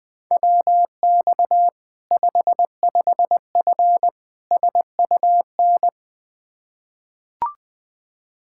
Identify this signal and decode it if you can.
Morse code, heard as a keyed tone, about 20 wpm, 700 Hz. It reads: WX 55F SUN
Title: Practice QSO - Exchange 2 - Info - 20wpm